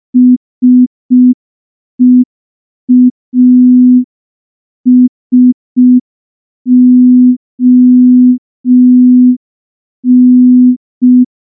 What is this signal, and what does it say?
Morse code, heard as a keyed tone, about 5 wpm, 250 Hz. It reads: SEASON
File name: morse / morse-training-05WPM